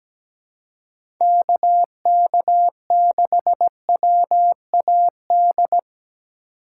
Morse code, heard as a keyed tone, about 17 wpm, 700 Hz. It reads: KK6WAD
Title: USA Call Signs - 6 Characters - 4x Character Spacing - 17wpm